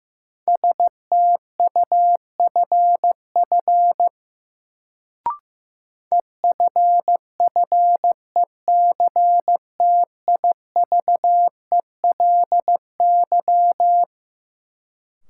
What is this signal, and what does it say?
Morse code, heard as a keyed tone, about 15 wpm, 700 Hz. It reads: STUFF EFFECTIVELY
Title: Sentences from Top 1400 Words - New Word Drill - 15wpm